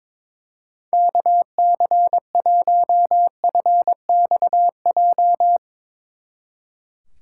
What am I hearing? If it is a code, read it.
Morse code, heard as a keyed tone, about 24 wpm, 700 Hz. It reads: KC1FXJ
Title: Speed Racing - USA 2x3 Call Signs 22wpm